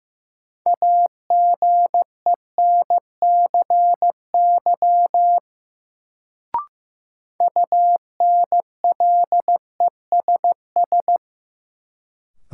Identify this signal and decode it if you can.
Morse code, heard as a keyed tone, about 15 wpm, 700 Hz. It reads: AGENCY UNLESS
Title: Sentences from Top 1300 Words - New Word Drill Alt - 15wpm